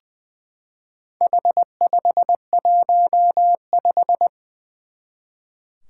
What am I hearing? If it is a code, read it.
Morse code, heard as a keyed tone, about 20 wpm, 700 Hz. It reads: H515
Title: Four Challenging Letter-Number Combinations 20wpm